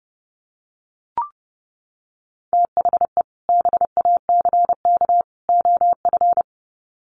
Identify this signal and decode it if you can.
Morse code, heard as a keyed tone, about 30 wpm, 700 Hz. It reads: THE BACK OF
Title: Sets of 3 Words 30wpm